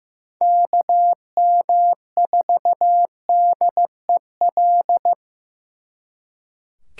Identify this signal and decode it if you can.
Morse code, heard as a keyed tone, about 18 wpm, 700 Hz. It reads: KM4DEL
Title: Speed Racing - USA 2x3 Call Signs 15wpm